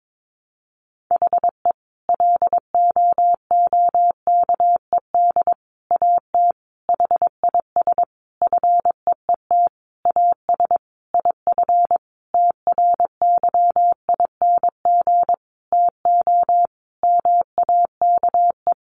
Morse code, heard as a keyed tone, about 22 wpm, 700 Hz. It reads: HE LOOKED AT HIS FEET AS IF TRYING TO MAKE
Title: A Sound Of Thunder 22wpm